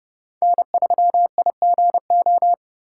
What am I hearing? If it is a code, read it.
Morse code, heard as a keyed tone, about 29 wpm, 700 Hz. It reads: N3IGO